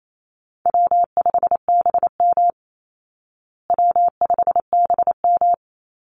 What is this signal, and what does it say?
Morse code, heard as a keyed tone, about 28 wpm, 700 Hz. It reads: W5BM W5BM
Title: Easy Call Signs Repeated 2x - 28wpm